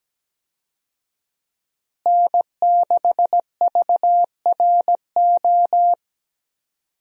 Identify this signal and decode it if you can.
Morse code, heard as a keyed tone, about 17 wpm, 700 Hz. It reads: N6VRO
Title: USA Call Signs - 5 Characters - Repeated 3x - 17wpm